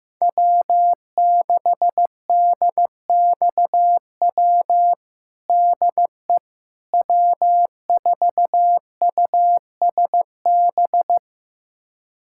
Morse code, heard as a keyed tone, about 15 wpm, 700 Hz. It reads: W6DXW DE W4USB
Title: USA Callsign DE Callsign - 5 Characters - 15wpm